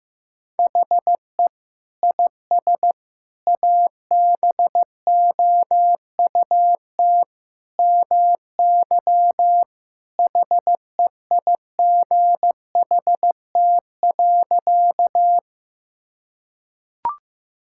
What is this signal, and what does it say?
Morse code, heard as a keyed tone, about 15 wpm, 700 Hz. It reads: HE IS ABOUT MY HEIGHT.
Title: Sentences from Top 1600 Words 15wpm